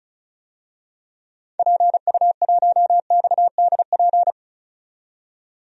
Morse code, heard as a keyed tone, about 36 wpm, 700 Hz. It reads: PU1XDP
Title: DXCC-100 Call Signs - 6 Characters - 35wpm